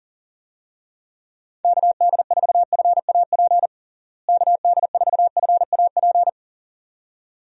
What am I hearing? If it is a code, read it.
Morse code, heard as a keyed tone, about 40 wpm, 700 Hz. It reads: KD4FAP KD4FAP
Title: USA Call Signs - 6 Characters - Repeated 2x - 40wpm